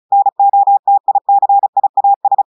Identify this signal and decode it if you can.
Morse code, heard as a keyed tone, about 36 wpm, 800 Hz. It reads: NOTICIAS
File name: portugais texte en clair 2 vitesse 35 mots minutes